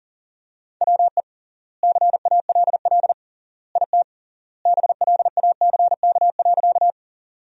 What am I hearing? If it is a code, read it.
Morse code, heard as a keyed tone, about 40 wpm, 700 Hz. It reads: WE CALL IT BLACK.
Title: 3x Word-Spacing - Sentences from Top 400 Words - 40wpm